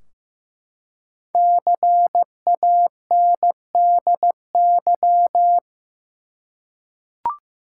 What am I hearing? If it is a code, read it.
Morse code, heard as a keyed tone, about 15 wpm, 700 Hz. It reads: CANDY
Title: Five Letter Words - 15wpm